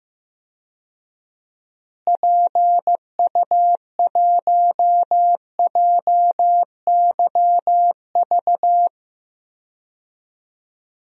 Morse code, heard as a keyed tone, about 15 wpm, 700 Hz. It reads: PU1JYV
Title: DXCC-100 Call Signs - 6 Characters - Repeated 2x - 15wpm